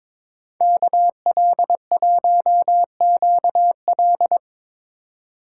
Morse code, heard as a keyed tone, about 24 wpm, 700 Hz. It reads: KL1QL
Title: Speed Racing - USA 2x2 Call Signs 22wpm